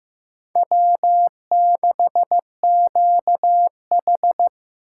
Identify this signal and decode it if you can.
Morse code, heard as a keyed tone, about 18 wpm, 700 Hz. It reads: W6QH
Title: Speed Racing - USA 1x2 Call Signs 15wpm